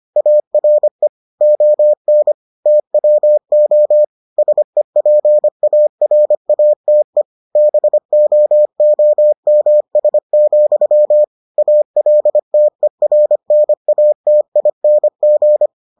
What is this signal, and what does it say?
Morse code, heard as a keyed tone, about 25 wpm, 600 Hz. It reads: ARE ON TWO SEPARATE BOOMS, ALTERNATING